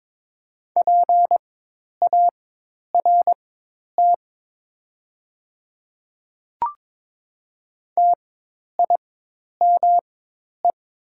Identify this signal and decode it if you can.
Morse code, heard as a keyed tone, about 22 wpm, spaced as for 10 wpm, 700 Hz. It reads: PART TIME
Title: 4x Character Spacing - Top 200 Words - 22wpm